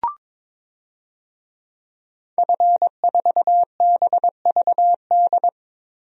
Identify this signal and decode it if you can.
Morse code, heard as a keyed tone, about 22 wpm, 700 Hz. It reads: F4BVD
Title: DXCC-100 Call Signs - 5 Characters - 22wpm